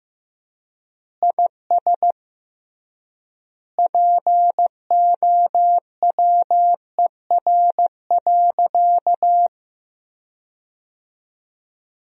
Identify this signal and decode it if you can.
Morse code, heard as a keyed tone, about 16 wpm, 700 Hz. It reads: IS POWER.
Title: 3x Word-Spacing - Sentences from Top 500 Words - 15wpm